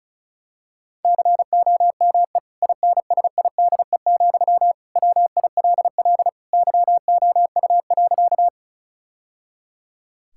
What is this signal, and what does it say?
Morse code, heard as a keyed tone, about 36 wpm, 700 Hz. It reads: COME INSIDE, WILL YOU.
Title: Sentences from Top 1000 Words - Review Part 6 - 35wpm